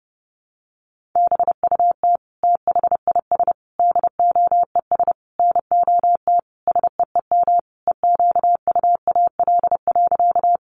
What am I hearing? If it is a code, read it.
Morse code, heard as a keyed tone, about 28 wpm, 700 Hz. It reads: BUT THIS DOES NOT SEEM EQUAL.
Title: Sentences from Top 1400 Words 30wpm